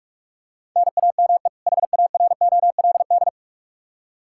Morse code, heard as a keyed tone, about 46 wpm, 700 Hz. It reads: NAME HAROLD
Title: Practice QSO - Exchange 1 - Beginning - 45wpm